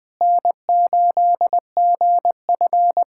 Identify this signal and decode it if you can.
Morse code, heard as a keyed tone, about 21 wpm, 700 Hz. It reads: N8GF
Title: USA Callsign DE Callsign 20wpm